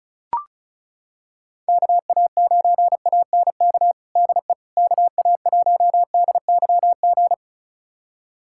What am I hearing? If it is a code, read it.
Morse code, heard as a keyed tone, about 34 wpm, 700 Hz. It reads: KA9ANK DE KA1DYG
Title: USA Callsign DE Callsign - 6 Characters - 35wpm